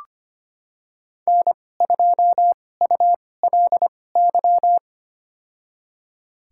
Morse code, heard as a keyed tone, about 25 wpm, spaced as for 18 wpm, 700 Hz. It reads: N2ULY
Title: USA Call Signs - 5 Characters - 2x Character Spacing - 25wpm